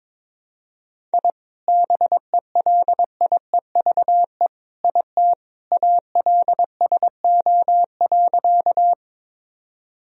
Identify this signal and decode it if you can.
Morse code, heard as a keyed tone, about 22 wpm, 700 Hz. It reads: I BELIEVE IT ALSO.
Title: Sentences from Top 1000 Words - Review Part 1 - 22wpm